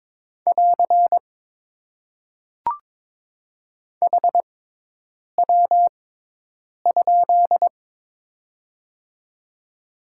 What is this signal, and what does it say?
Morse code, heard as a keyed tone, about 22 wpm, spaced as for 8 wpm, 700 Hz. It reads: <AR> HW?
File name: Top 90 QSO Elements - 6x Character Spacing - 22wpm